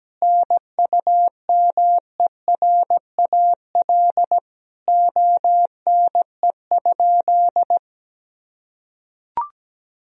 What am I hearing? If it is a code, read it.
Morse code, heard as a keyed tone, about 17 wpm, 700 Hz. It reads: NUMERAL ONE?
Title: Sentences from Top 1000 Words - Review Part 2 - 17wpm